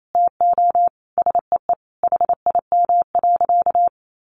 Morse code, heard as a keyed tone, about 28 wpm, 700 Hz. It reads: TO SEE HIM.
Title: Sentences from Top 900 Words 28wpm